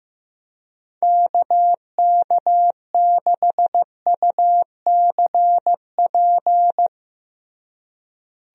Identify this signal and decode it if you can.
Morse code, heard as a keyed tone, about 16 wpm, 700 Hz. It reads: KK6UCP
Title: USA Call Signs - 6 Characters - Repeated 2x - 15wpm